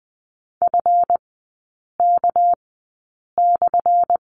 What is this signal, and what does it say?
Morse code, heard as a keyed tone, about 20 wpm, spaced as for 8 wpm, 700 Hz. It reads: FK/
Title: 212 Lesson 34e - Number 8 - Three Characters 20wpm